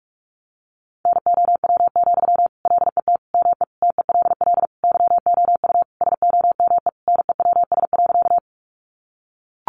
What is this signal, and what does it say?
Morse code, heard as a keyed tone, about 44 wpm, 700 Hz. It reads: NOW, LET ME TELL YOU SOME NEWS.
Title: Sentences from Top 800 Words 45wpm